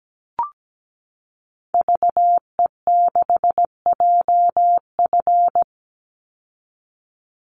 Morse code, heard as a keyed tone, about 17 wpm, 700 Hz. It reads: VE6JF
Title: Call Signs - 17wpm